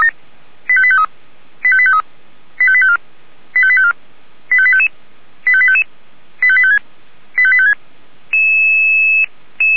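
Five-tone selective call (ZVEI), heard as quick runs of five tones, tones of about 70 ms. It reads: 85853 85853 85854 85854 85855 85855 85856 85856
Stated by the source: ZVEI five-tone selective call 85738, received on SDR.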